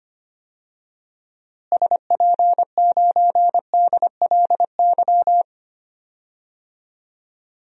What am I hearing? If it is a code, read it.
Morse code, heard as a keyed tone, about 25 wpm, 700 Hz. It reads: SP9DLY
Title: DXCC-100 Call Signs - 6 Characters - Repeated 3x - 25wpm